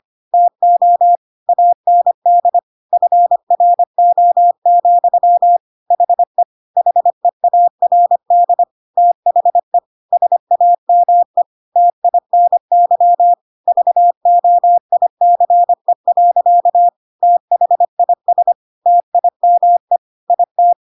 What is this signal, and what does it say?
Morse code, heard as a keyed tone, about 25 wpm, 700 Hz. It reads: TO AND FRO, HE HEARD THE SAME TINY VOICE. THIS TIME IT
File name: Pinocchio / Pinocchio ch1 25 25 ik1hge